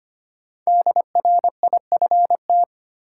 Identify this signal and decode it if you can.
Morse code, heard as a keyed tone, about 25 wpm, 700 Hz. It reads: DRIFT
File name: 5 Letter Words - First 3 Letters Given - 25wpm